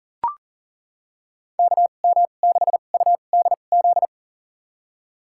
Morse code, heard as a keyed tone, about 39 wpm, spaced as for 29 wpm, 700 Hz. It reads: KM6UDZ
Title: USA Call Signs - 6 Characters - 2x Character Spacing - 40wpm